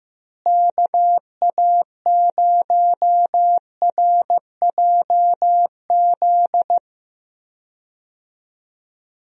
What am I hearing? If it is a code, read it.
Morse code, heard as a keyed tone, about 15 wpm, 700 Hz. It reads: KA0RJZ
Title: USA Call Signs - 6 Characters - Repeated 3x - 15wpm